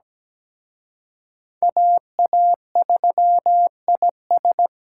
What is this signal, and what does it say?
Morse code, heard as a keyed tone, about 20 wpm, 700 Hz. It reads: AA3IS
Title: Speed Racing - USA 2x2 Call Signs 17wpm